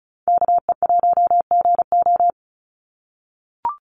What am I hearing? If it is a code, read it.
Morse code, heard as a keyed tone, about 34 wpm, 700 Hz. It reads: KE1GO
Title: USA Call Signs 35wpm